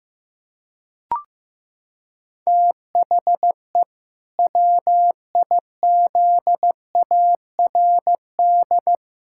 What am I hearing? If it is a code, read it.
Morse code, heard as a keyed tone, about 15 wpm, 700 Hz. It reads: THE WIZARD
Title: Pangram Challenge v2 - 15wpm